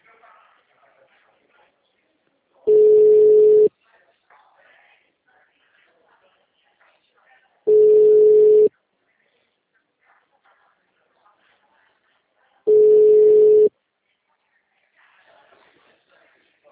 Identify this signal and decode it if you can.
Ringback tone (Europe, 425 Hz).